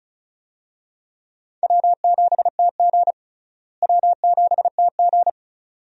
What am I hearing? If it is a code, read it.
Morse code, heard as a keyed tone, about 36 wpm, 700 Hz. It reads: W7TG W7TG
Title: USA Call Signs - 4 Characters - Repeated 2x - 35wpm